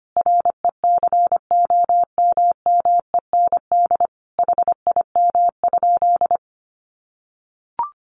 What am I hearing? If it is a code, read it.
Morse code, heard as a keyed tone, about 25 wpm, 700 Hz. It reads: RECOMMEND HIM?
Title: Sentences from Top 1200 Words 25wpm